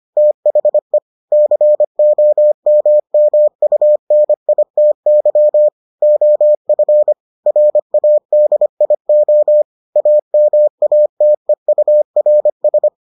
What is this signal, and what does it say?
Morse code, heard as a keyed tone, about 25 wpm, 600 Hz. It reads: THE COMMUNITY OF RADIO AMATEURS